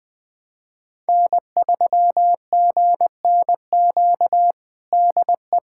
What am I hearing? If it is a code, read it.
Morse code, heard as a keyed tone, about 20 wpm, 700 Hz. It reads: N3GNQ DE
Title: USA Callsign DE Callsign - 5 Characters - 20wpm